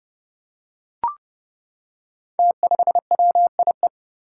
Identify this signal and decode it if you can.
Morse code, heard as a keyed tone, about 30 wpm, 700 Hz. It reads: T5WIE
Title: Five Challenging Letter-Number Combinations 30wpm